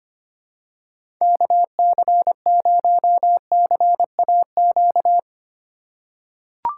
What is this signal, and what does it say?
Morse code, heard as a keyed tone, about 25 wpm, 700 Hz. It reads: KC0CAQ
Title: USA Call Signs - 6 Characters - 2x Character Spacing - 25wpm